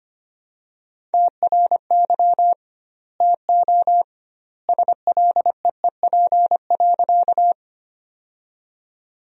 Morse code, heard as a keyed tone, about 25 wpm, 700 Hz. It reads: TRY TO SLEEP.
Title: 2x Word-Spacing - Sentences from Top 500 Words - 25wpm